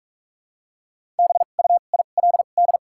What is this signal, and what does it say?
Morse code, heard as a keyed tone, about 42 wpm, spaced as for 31 wpm, 700 Hz. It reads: BUILD
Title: 2x Character Spacing - Top 300 Words - 45wpm